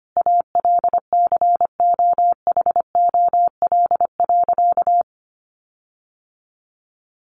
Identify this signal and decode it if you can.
Morse code, heard as a keyed tone, about 25 wpm, 700 Hz. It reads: ALCOHOL.
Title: Sentences from Top 1400 Words 25wpm